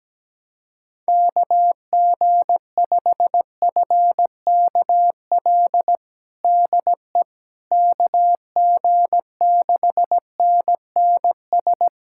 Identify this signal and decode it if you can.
Morse code, heard as a keyed tone, about 17 wpm, 700 Hz. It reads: KG5FKL DE KG6NNS
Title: USA Callsign DE Callsign 17wpm